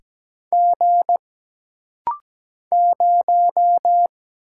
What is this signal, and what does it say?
Morse code, heard as a keyed tone, about 17 wpm, 700 Hz. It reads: G 0
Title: Single Letter-Number - Rapid-Fire 17wpm